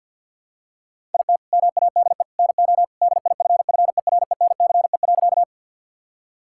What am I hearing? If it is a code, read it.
Morse code, heard as a keyed tone, about 50 wpm, 700 Hz. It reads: IT MADE NO DIFFERENCE.
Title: Sentences from Top 1000 Words - Review Part 5 - 50wpm